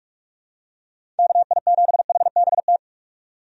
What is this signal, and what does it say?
Morse code, heard as a keyed tone, about 47 wpm, 700 Hz. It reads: KI7HBT
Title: USA Call Signs - 6 Characters - 45wpm